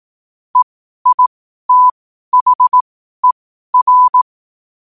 Morse code, heard as a keyed tone, about 18 wpm, spaced as for 12 wpm, 1000 Hz. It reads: EITHER
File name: qotdfiles / 12wpm